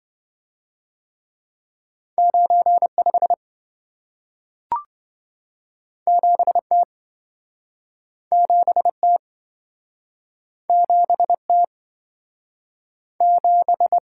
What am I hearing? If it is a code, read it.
Morse code, heard as a keyed tone, about 28 wpm, 700 Hz. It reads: 95 7T 7T 7T 7